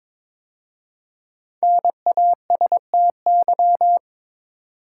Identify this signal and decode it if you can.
Morse code, heard as a keyed tone, about 22 wpm, 700 Hz. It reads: NASTY
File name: 5 Letter Words - First 3 Letters Given - 22wpm